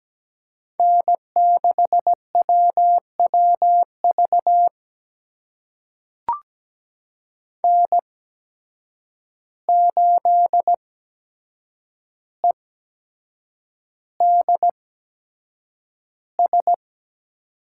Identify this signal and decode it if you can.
Morse code, heard as a keyed tone, about 17 wpm, spaced as for 7 wpm, 700 Hz. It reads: N6WWV N8EDS
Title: USA Call Signs - 8x Character Spacing - 17wpm